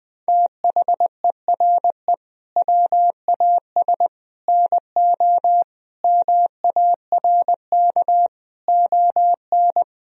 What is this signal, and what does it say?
Morse code, heard as a keyed tone, about 20 wpm, 700 Hz. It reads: THERE WAS NO MARK ON
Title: Sentences from Top 1000 Words - Review Part 1 - 20wpm